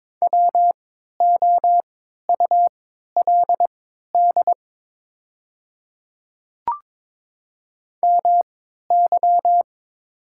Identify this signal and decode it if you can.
Morse code, heard as a keyed tone, about 22 wpm, spaced as for 12 wpm, 690 Hz. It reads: WOULD MY
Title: 3x Character Spacing - Top 100 Words - 22wpm